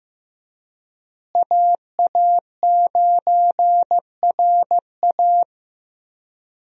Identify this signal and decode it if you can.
Morse code, heard as a keyed tone, about 15 wpm, 700 Hz. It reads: AA9RA
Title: USA Call Signs - 5 Characters - Repeated 2x - 15wpm